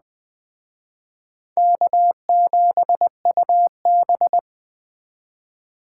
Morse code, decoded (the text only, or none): K7UB